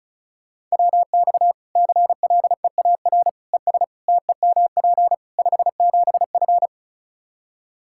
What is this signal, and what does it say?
Morse code, heard as a keyed tone, about 36 wpm, 700 Hz. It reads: WX CLEAR ES TEMP 57F
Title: Practice QSO - Exchange 2 - Info - 35wpm